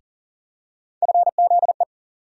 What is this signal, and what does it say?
Morse code, heard as a keyed tone, about 40 wpm, 700 Hz. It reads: FZE
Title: USA Call Sign Postfixes - 3 Letters 40wpm